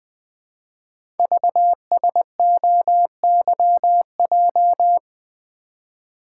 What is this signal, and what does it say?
Morse code, heard as a keyed tone, about 20 wpm, 700 Hz. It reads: VSOYJ
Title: Code Groups - 5 Letters - 20wpm